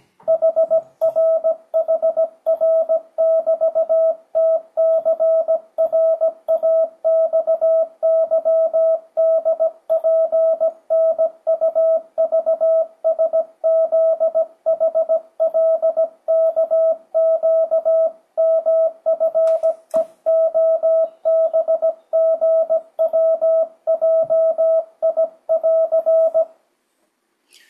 Morse code, heard as a keyed tone, about 17 wpm, 650 Hz, with radio noise. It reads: HRHR<BT>TCRAXYDPNUVSZHLKQMFEOBGWJI<AR>
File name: 2025cw / speedup / 17wpm